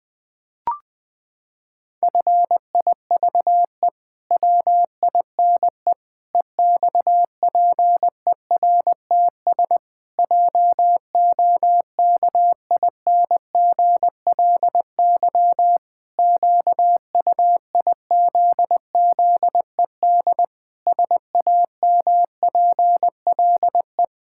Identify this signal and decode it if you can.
Morse code, heard as a keyed tone, about 20 wpm, 700 Hz. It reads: FIVE WINE EXPERTS JOKINGLY QUIZZED SAMPLE
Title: Pangram Challenge v2 - 20wpm